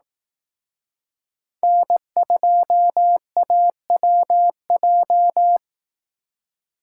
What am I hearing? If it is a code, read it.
Morse code, heard as a keyed tone, about 18 wpm, 700 Hz. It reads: N2AWJ
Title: Speed Racing - USA 1x3 Call Signs 15wpm